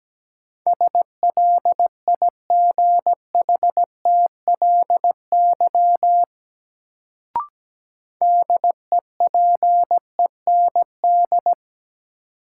Morse code, heard as a keyed tone, about 17 wpm, 700 Hz. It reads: SLIGHTLY DEPEND
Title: Sentences from Top 1300 Words - New Word Drill - 17wpm